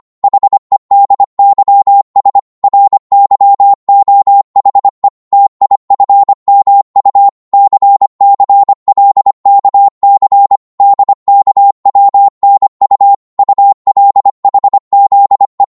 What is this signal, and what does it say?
Morse code, heard as a keyed tone, about 25 wpm, 800 Hz. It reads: HEDYS RYOHE TIFMU CCLKC DKWNU ULHZE
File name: Morse / ltr-25wpm